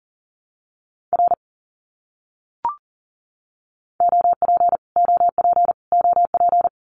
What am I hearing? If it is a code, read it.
Morse code, heard as a keyed tone, about 40 wpm, 700 Hz. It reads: R OP OP OP
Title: Abbreviations Repeated 3x - 40wpm